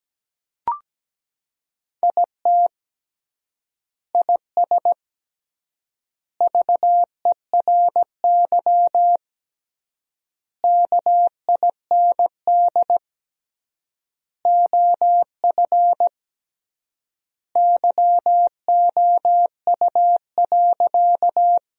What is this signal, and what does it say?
Morse code, heard as a keyed tone, about 17 wpm, 700 Hz. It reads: IT IS VERY KIND OF YOU.
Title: 3x Word-Spacing - Sentences from Top 200 Words - 17wpm